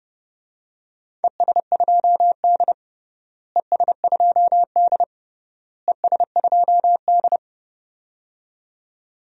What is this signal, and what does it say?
Morse code, heard as a keyed tone, about 30 wpm, 700 Hz. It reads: ES2D ES2D ES2D
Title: DXCC-100 Call Signs - 4 Characters - Repeated 3x - 30wpm